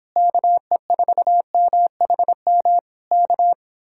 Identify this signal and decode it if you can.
Morse code, heard as a keyed tone, about 26 wpm, 700 Hz. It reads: KE4MHM K